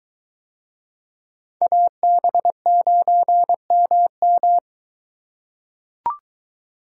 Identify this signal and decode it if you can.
Morse code, heard as a keyed tone, about 23 wpm, 700 Hz. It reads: AB9MM